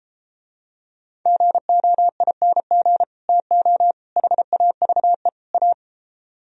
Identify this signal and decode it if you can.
Morse code, heard as a keyed tone, about 33 wpm, 700 Hz. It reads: GOING TO HAVE A